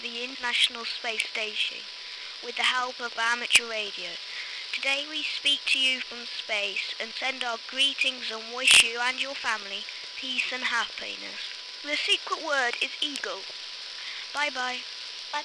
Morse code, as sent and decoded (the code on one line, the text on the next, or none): none